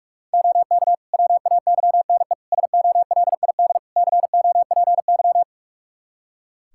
Morse code, heard as a keyed tone, about 45 wpm, 700 Hz. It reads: OK WAYNE SOLID COPY
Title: Practice QSO - Exchange 3 - More Info - 45wpm